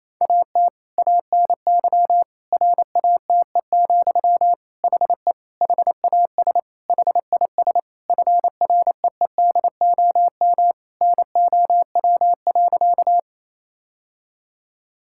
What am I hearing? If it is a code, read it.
Morse code, heard as a keyed tone, about 27 wpm, 700 Hz. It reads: AT ANY RATE, HE HAS HIS FREEDOM NOW.
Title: Sentences from Top 1000 Words - Review Part 5 - 28wpm